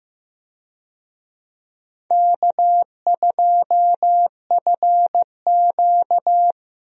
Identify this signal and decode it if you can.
Morse code, heard as a keyed tone, about 15 wpm, 700 Hz. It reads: K2FQ
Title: USA Call Signs - 4 Characters - Repeated 3x - 15wpm